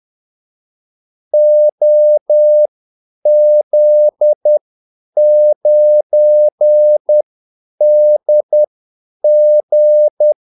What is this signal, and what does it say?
Morse code, heard as a keyed tone, about 10 wpm, spaced as for 8 wpm, 600 Hz. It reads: OZ9DG
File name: Callsigns-10-8wpm